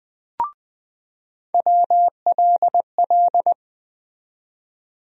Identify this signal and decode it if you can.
Morse code, heard as a keyed tone, about 20 wpm, 700 Hz. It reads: WLL
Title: USA Call Sign Postfixes - 3 Letters 20wpm